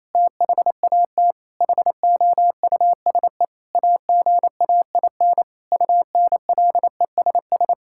Morse code, heard as a keyed tone, about 28 wpm, 700 Hz. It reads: THAT HOUSE AGAIN UNLESS
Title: Sentences from Top 1000 Words - Review Part 6 - 28wpm